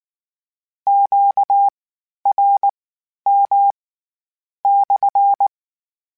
Morse code, heard as a keyed tone, about 19 wpm, spaced as for 9 wpm, 800 Hz. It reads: QRM/
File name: Q Calls / Q Calls-mp3s / Q Calls SPSL1 @18